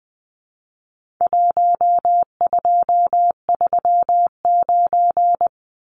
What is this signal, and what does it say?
Morse code, heard as a keyed tone, about 20 wpm, 700 Hz. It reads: 1239
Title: Numbers - 4-Digit 20wpm